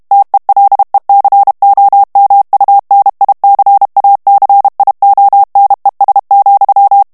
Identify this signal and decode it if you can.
Morse code, heard as a keyed tone, about 32 wpm, 800 Hz. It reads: TELECOMUNICACIONES,
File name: ESPAGNOL TEXTE EN CLAIR 2 VITESSE 30 MOTS MINUTES